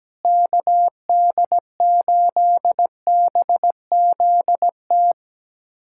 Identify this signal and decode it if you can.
Morse code, heard as a keyed tone, about 17 wpm, 700 Hz. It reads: KD8BZT
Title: USA Callsign DE Callsign 17wpm